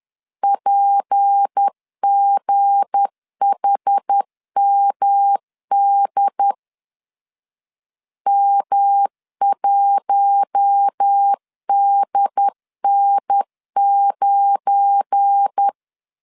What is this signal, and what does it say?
Morse code, heard as a keyed tone, about 10 wpm, 800 Hz. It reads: PGHMD M1DN9
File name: LEZIONI / 7 GF9Z / 7 GF9 10wpm 2747